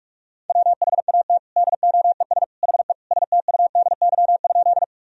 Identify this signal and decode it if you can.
Morse code, heard as a keyed tone, about 42 wpm, 700 Hz. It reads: WHAT DOES HE STUDY?
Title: Sentences from Top 1000 Words - Review Part 1 - 45wpm